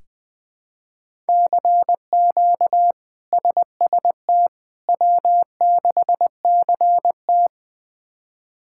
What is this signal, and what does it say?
Morse code, heard as a keyed tone, about 21 wpm, 700 Hz. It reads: CQ SST W6CT
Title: K1USN SST QSO Practice - 20wpm